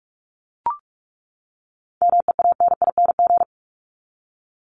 Morse code, heard as a keyed tone, about 44 wpm, 700 Hz. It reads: MEANING
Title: Top 751-1000 Words 45wpm